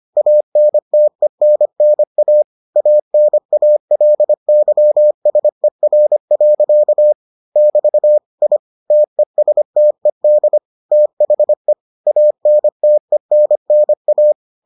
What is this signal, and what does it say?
Morse code, heard as a keyed tone, about 25 wpm, 600 Hz. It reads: ANTENNA ANALYSER. <BT> I TESTED THE ANTENNA